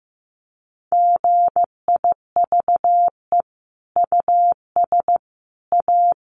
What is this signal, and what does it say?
Morse code, heard as a keyed tone, about 15 wpm, 700 Hz. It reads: GIVE US A
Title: Sentences from Top 1500 Words 15wpm